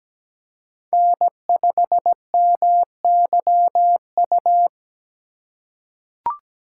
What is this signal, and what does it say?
Morse code, heard as a keyed tone, about 17 wpm, 700 Hz. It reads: N5MYU
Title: USA Call Signs - 5 Characters - 2x Character Spacing - 17wpm